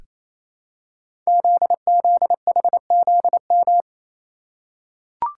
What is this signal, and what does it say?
Morse code, heard as a keyed tone, about 27 wpm, 700 Hz. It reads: ZZHZM
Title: Code Groups - 5 Letters - 28wpm